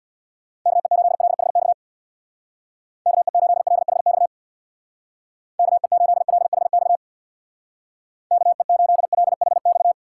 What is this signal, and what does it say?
Morse code, heard as a keyed tone, about 61 wpm, 700 Hz. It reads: KE8LHX KE8LHX KE8LHX KE8LHX
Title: Speed Racing - USA Call Signs 50wpm